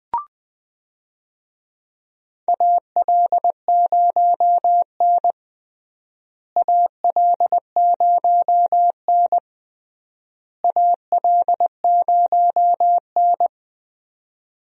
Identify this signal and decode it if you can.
Morse code, heard as a keyed tone, about 20 wpm, 700 Hz. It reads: AL0N AL0N AL0N
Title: USA Call Signs - 4 Characters - Repeated 3x - 20wpm